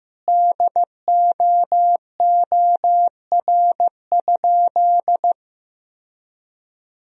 Morse code, heard as a keyed tone, about 15 wpm, 700 Hz. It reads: DOOR?
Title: Sentences from Top 1000 Words - Review Part 4 - 15wpm